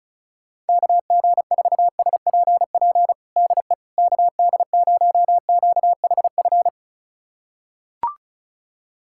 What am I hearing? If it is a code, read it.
Morse code, heard as a keyed tone, about 34 wpm, 700 Hz. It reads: KG4SPP DE KD0QHF
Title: USA Callsign DE Callsign - 6 Characters - 35wpm